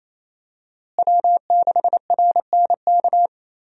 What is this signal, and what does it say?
Morse code, heard as a keyed tone, about 28 wpm, 700 Hz. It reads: W6RNK
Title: USA Call Signs - 5 Characters - Repeated 2x - 28wpm